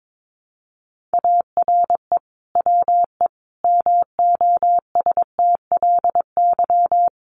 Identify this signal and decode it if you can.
Morse code, heard as a keyed tone, about 22 wpm, 700 Hz. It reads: ARE WE MOSTLY
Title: Sentences from Top 1300 Words 22wpm